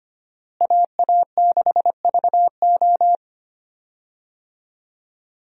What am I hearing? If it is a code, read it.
Morse code, heard as a keyed tone, about 25 wpm, 700 Hz. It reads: AA6VO
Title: USA Call Signs - 5 Characters - Repeated 2x - 25wpm